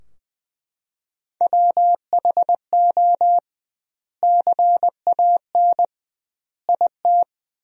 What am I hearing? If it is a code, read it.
Morse code, heard as a keyed tone, about 20 wpm, 700 Hz. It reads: WHO CAN IT
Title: 2x Word-Spacing - Sentences from Top 100 Words - 20wpm